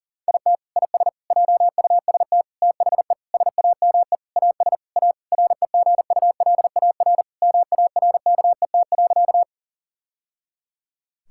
Morse code, heard as a keyed tone, about 37 wpm, 700 Hz. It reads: IT IS JUST THE SAME AS A REGULAR MARKET.
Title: Sentences from Top 1000 Words - Review Part 6 - 40wpm